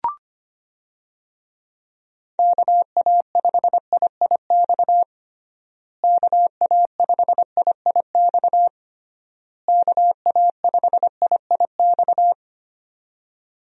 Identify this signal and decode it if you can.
Morse code, heard as a keyed tone, about 25 wpm, 700 Hz. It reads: KA5IIX KA5IIX KA5IIX
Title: USA Call Signs - 6 Characters - Repeated 3x - 25wpm